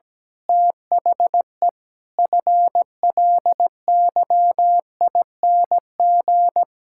Morse code, heard as a keyed tone, about 17 wpm, 700 Hz. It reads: THE FLYING
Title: Pangram Challenge v2 - 17wpm